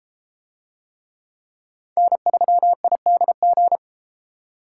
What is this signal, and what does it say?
Morse code, heard as a keyed tone, about 31 wpm, 700 Hz. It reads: N3IDG